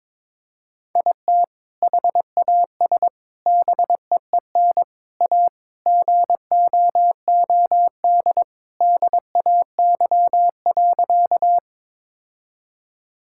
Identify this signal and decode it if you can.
Morse code, heard as a keyed tone, about 21 wpm, 700 Hz. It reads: IT HAS BEEN A GOOD DAY.
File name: practice / 22wpm / Sentences with 2-Word Phrases - 22wpm